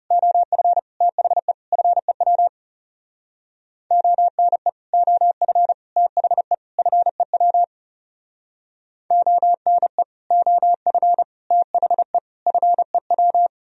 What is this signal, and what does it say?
Morse code, heard as a keyed tone, about 37 wpm, 700 Hz. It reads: OF THE FEW ONE OF THE FEW ONE OF THE FEW
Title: Speed-Racing - Sets of 4 Words 30wpm